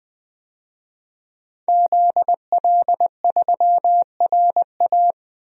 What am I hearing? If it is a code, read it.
Morse code, heard as a keyed tone, about 20 wpm, 700 Hz. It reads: ZL3RA
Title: DXCC-100 Call Signs - 5 Characters - 20wpm